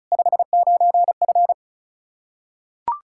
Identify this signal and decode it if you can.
Morse code, heard as a keyed tone, about 34 wpm, 700 Hz. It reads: 59F
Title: K1USN SST QSO Practice 35wpm